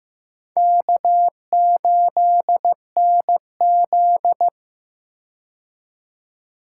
Morse code, heard as a keyed tone, about 15 wpm, 700 Hz. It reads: K8NZ
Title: K1USN SST QSO Practice - 15wpm